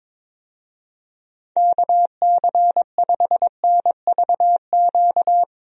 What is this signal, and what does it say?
Morse code, heard as a keyed tone, about 21 wpm, 700 Hz. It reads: KC5NVQ
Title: USA Call Signs - 6 Characters - Repeated 2x - 22wpm